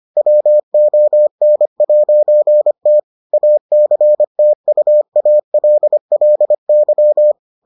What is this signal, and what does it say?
Morse code, heard as a keyed tone, about 25 wpm, 600 Hz. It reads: WON'T ACTUALLY